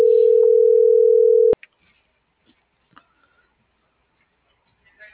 Ringback tone (North America), 440 + 480 Hz.
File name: CENTRAL MAERSK / out kevin.riosmsk